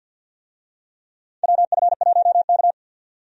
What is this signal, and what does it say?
Morse code, heard as a keyed tone, about 52 wpm, 700 Hz. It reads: WF1K
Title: USA Call Signs - 4 Characters - 50wpm